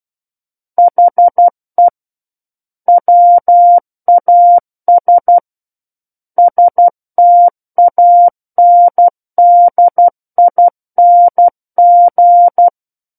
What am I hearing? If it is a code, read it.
Morse code, heard as a keyed tone, about 12 wpm, 700 Hz. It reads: HE WAS STANDING
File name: Book One / 03-Chapter 3 - 12 WPM